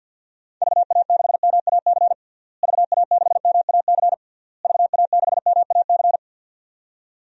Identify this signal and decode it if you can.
Morse code, heard as a keyed tone, about 52 wpm, 700 Hz. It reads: VA6MAC VA6MAC VA6MAC
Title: DXCC-100 Call Signs - 6 Characters - Repeated 3x - 50wpm